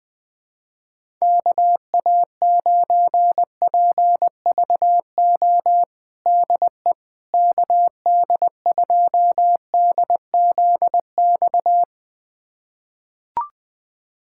Morse code, heard as a keyed tone, about 20 wpm, 700 Hz. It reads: KA9PVO DE KD2DZX
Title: USA Callsign DE Callsign - 6 Characters - 20wpm